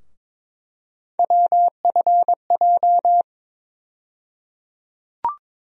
Morse code, heard as a keyed tone, about 22 wpm, 700 Hz. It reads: WFJ
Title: USA Call Sign Postfixes - 3 Letters 22wpm